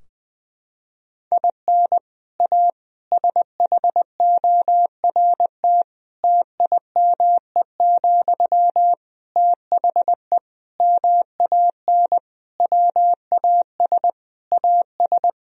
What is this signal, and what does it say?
Morse code, heard as a keyed tone, about 20 wpm, 700 Hz. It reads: IN A SHORT TIME, THE MAN WAS AS
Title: Sentences from Top 1000 Words - Review Part 2 - 20wpm